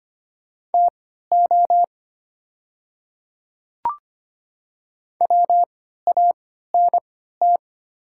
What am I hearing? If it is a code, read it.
Morse code, heard as a keyed tone, about 25 wpm, spaced as for 14 wpm, 690 Hz. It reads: TO WANT
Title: 3x Character Spacing - Top 200 Words - 25wpm